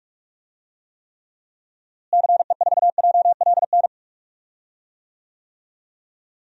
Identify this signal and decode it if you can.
Morse code, heard as a keyed tone, about 46 wpm, 700 Hz. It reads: CE4JLN
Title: DXCC-100 Call Signs - 6 Characters - Repeated 3x - 45wpm